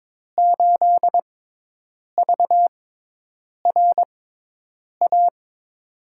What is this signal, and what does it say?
Morse code, heard as a keyed tone, about 22 wpm, spaced as for 8 wpm, 700 Hz. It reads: 8VRA
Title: USA Call Signs - 6 Characters - 6x Character Spacing - 22wpm